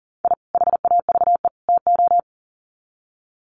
Morse code, heard as a keyed tone, about 40 wpm, 700 Hz. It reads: I HAVE TO
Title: Sets of 3 Words 40wpm